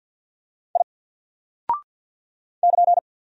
Morse code, heard as a keyed tone, about 48 wpm, 700 Hz. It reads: I <KN>
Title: Single Character (Letter:Number:Punctuation:Prosign) 50wpm